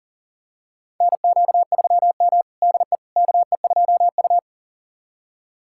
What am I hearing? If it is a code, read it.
Morse code, heard as a keyed tone, about 40 wpm, 700 Hz. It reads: NQ3M DE KE2U
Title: USA Callsign DE Callsign - 4 Characters - 40wpm